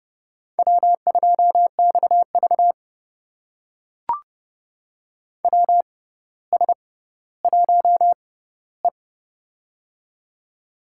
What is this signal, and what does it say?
Morse code, heard as a keyed tone, about 30 wpm, spaced as for 15 wpm, 700 Hz. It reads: W2XV WS1E
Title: USA Call Signs - 4 Characters - 6x Character Spacing - 30wpm